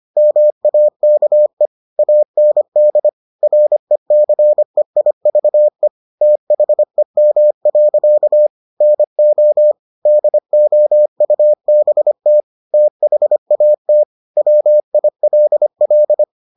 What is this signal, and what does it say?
Morse code, heard as a keyed tone, about 25 wpm, 600 Hz. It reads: MAKE AND RECEIVE THEM. NO DOUBT THAT WILL